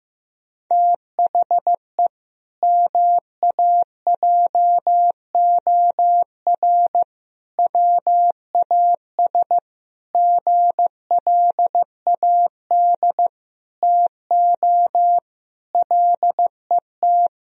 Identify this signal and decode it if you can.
Morse code, heard as a keyed tone, about 15 wpm, 700 Hz. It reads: THE MAJOR WAS GLAD TO LET
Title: Sentences from Top 1000 Words - Review Part 4 - 15wpm